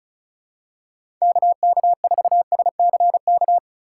Morse code, heard as a keyed tone, about 34 wpm, 700 Hz. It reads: KK4SCK
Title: USA Call Signs - 6 Characters - 2x Character Spacing - 35wpm